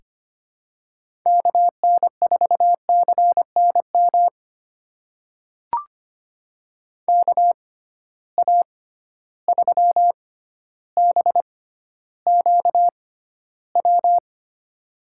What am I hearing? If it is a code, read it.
Morse code, heard as a keyed tone, about 25 wpm, spaced as for 13 wpm, 700 Hz. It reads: KN4CNM KA3BQW
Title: USA Call Signs - 6 Characters - 6x Character Spacing - 25wpm